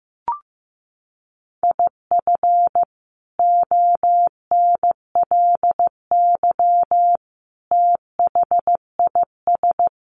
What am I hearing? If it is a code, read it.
Morse code, heard as a keyed tone, about 15 wpm, 700 Hz. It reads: IF ONLY THIS
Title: Sentences from Top 1500 Words 15wpm